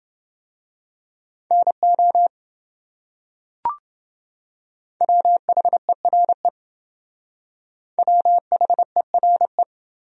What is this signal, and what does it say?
Morse code, heard as a keyed tone, about 28 wpm, 700 Hz. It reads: NO WHERE WHERE